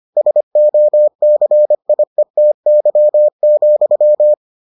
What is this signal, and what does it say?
Morse code, heard as a keyed tone, about 25 wpm, 600 Hz. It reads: SOCIETY,